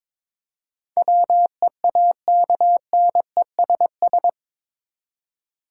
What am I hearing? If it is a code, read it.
Morse code, heard as a keyed tone, about 22 wpm, 700 Hz. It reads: WEAKNESS
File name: practice / 22wpm / Eight Letter Words - 22wpm